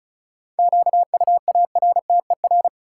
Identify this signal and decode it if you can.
Morse code, heard as a keyed tone, about 34 wpm, 700 Hz. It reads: QUARTER
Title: Sentences from Top 1300 Words - New Word Drill - 35wpm